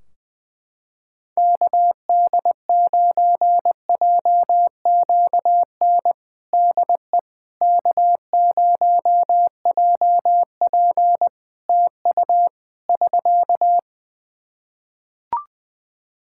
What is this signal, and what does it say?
Morse code, heard as a keyed tone, about 20 wpm, 700 Hz. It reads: KD9JQN DE K0JP TU <SK>
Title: Practice QSO - Exchange 4 - Ending - 20wpm